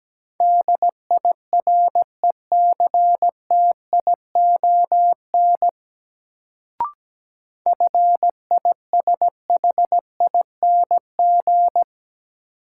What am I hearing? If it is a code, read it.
Morse code, heard as a keyed tone, about 17 wpm, 700 Hz. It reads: DIRECTION FISHING
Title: Sentences from Top 1000 Words - Review Part 5 - New Word Drill - 17wpm